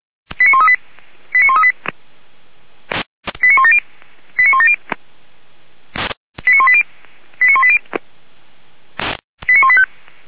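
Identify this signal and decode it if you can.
Five-tone selective call (ZVEI), heard as quick runs of five tones, tones of about 70 ms. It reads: 89138 89138 89170 89170 89180 89180 89185
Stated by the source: ZVEI five-tone selective call received on an SDR.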